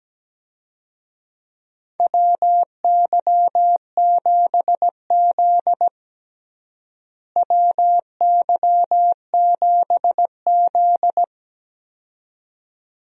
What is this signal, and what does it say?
Morse code, heard as a keyed tone, about 17 wpm, 700 Hz. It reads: WY7Z WY7Z
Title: USA Call Signs - 4 Characters - Repeated 2x - 17wpm